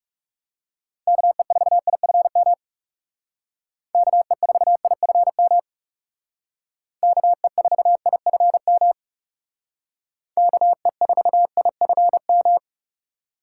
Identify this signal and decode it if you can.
Morse code, heard as a keyed tone, about 37 wpm, 700 Hz. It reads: KE4IFM KE4IFM KE4IFM KE4IFM
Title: Speed Racing - USA 2x3 Call Signs 30wpm